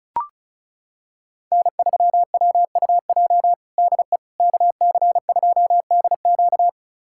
Morse code, heard as a keyed tone, about 36 wpm, 700 Hz. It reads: N3WUJ DE KC2DQ
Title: USA Callsign DE Callsign - 5 Characters - 35wpm